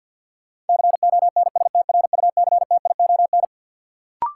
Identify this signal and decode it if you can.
Morse code, heard as a keyed tone, about 47 wpm, 700 Hz. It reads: CONSTRUCTION
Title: Sentences from Top 1600 Words - New Word Drill Alt - 50wpm